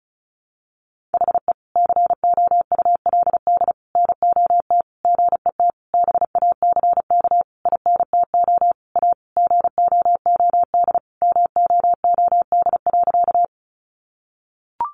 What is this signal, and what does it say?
Morse code, heard as a keyed tone, about 34 wpm, 700 Hz. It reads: HE COULD NOT GET BACK INTO A GOOD MOOD.
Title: Sentences from Top 1100 Words 35wpm